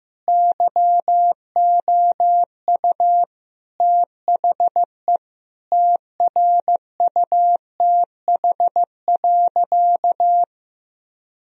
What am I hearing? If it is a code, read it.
Morse code, heard as a keyed tone, about 15 wpm, 700 Hz. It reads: YOU THE TRUTH.
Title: Sentences from Top 1000 Words - Review Part 2 - 15wpm